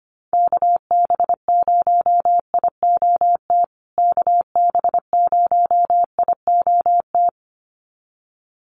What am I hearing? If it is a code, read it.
Morse code, heard as a keyed tone, about 25 wpm, 700 Hz. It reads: KB0IOT KB0IOT
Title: USA Call Signs Repeated 2x 25wpm